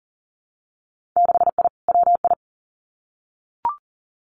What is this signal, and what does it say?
Morse code, heard as a keyed tone, about 40 wpm, 700 Hz. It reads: 6I WI
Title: Winter Field Day Exchanges Encore 40wpm